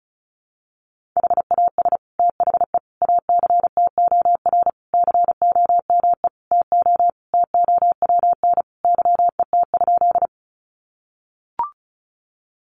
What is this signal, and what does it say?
Morse code, heard as a keyed tone, about 34 wpm, 700 Hz. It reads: HAS THE ACTOR COME TO TOWN YET?
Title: Sentences from Top 1500 Words 35wpm